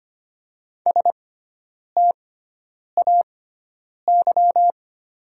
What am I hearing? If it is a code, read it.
Morse code, heard as a keyed tone, about 25 wpm, spaced as for 9 wpm, 700 Hz. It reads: STAY
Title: 6x Character Spacing - Top 500 Words - 25wpm